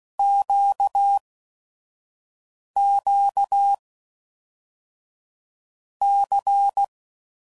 Morse code, heard as a keyed tone, about 16 wpm, 800 Hz. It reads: Q Q C
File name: LFCYQ@5 f15